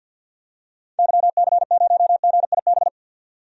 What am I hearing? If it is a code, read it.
Morse code, heard as a keyed tone, about 50 wpm, 700 Hz. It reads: YC0GIB
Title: DXCC-100 Call Signs - 6 Characters - 50wpm